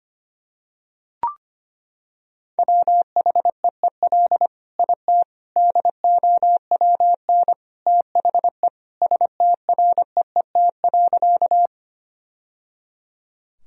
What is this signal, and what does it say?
Morse code, heard as a keyed tone, about 25 wpm, 700 Hz. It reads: WHEEL IT DOWN THE STREET.
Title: Sentences from Top 1000 Words - Review Part 3 - 25wpm